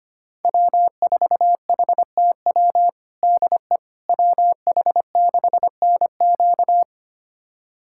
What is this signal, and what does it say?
Morse code, heard as a keyed tone, about 25 wpm, 700 Hz. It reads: W4HTW DE WH6NQ
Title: USA Callsign DE Callsign - 5 Characters - 25wpm